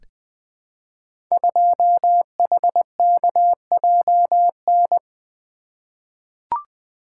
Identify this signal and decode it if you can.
Morse code, heard as a keyed tone, about 20 wpm, 700 Hz. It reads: 2HKJN
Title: Code Groups - 5 Letter:Numbers - 20wpm